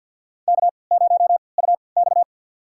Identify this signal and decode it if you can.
Morse code, heard as a keyed tone, about 50 wpm, spaced as for 28 wpm, 700 Hz. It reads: K0UX
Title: USA Call Signs - 4 Characters - 3x Character Spacing - 50wpm